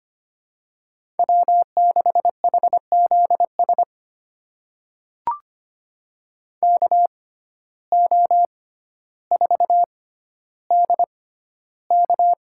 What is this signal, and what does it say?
Morse code, heard as a keyed tone, about 25 wpm, spaced as for 13 wpm, 700 Hz. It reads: W6HZS KO4DK
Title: USA Call Signs - 5 Characters - 6x Character Spacing - 25wpm